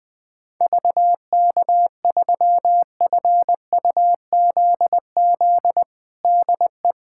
Morse code, heard as a keyed tone, about 20 wpm, 700 Hz. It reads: VK3FUZZ DE
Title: USA Callsign DE Callsign 20wpm